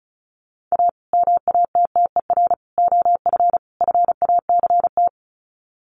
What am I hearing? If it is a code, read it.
Morse code, heard as a keyed tone, about 34 wpm, 700 Hz. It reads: A MATTER OF FACT
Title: Sets of 4 Words 35wpm